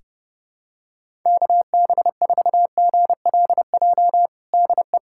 Morse code, heard as a keyed tone, about 30 wpm, 700 Hz. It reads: KB4GLJ DE
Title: USA Callsign DE Callsign - 6 Characters - 30wpm